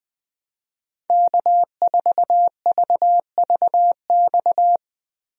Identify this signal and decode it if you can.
Morse code, heard as a keyed tone, about 20 wpm, 700 Hz. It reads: K4VVX
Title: USA Call Signs - 5 Characters - Repeated 3x - 20wpm